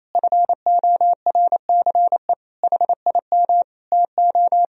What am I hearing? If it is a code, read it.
Morse code, heard as a keyed tone, about 28 wpm, 700 Hz. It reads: FORCE HIM TO
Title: Sentences from Top 1000 Words - Review Part 3 - 28wpm